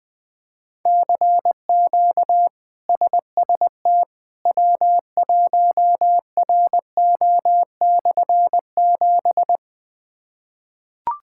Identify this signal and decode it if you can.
Morse code, heard as a keyed tone, about 21 wpm, 700 Hz. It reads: CQ SST W1RO/7
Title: K1USN SST QSO Practice - 20wpm